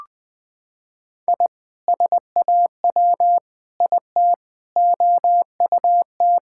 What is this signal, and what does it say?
Morse code, heard as a keyed tone, about 20 wpm, 700 Hz. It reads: I SAW IT OUT
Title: Sentences with 7-Word Phrases - 20wpm